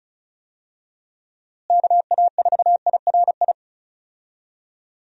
Morse code, heard as a keyed tone, about 36 wpm, 700 Hz. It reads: KA4IRI
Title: USA Call Signs - 6 Characters - Repeated 2x - 35wpm